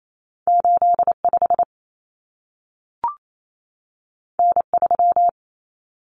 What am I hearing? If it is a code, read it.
Morse code, heard as a keyed tone, about 28 wpm, 700 Hz. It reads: 85 N3
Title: Common Cut Numbers - 2-Digit 28wpm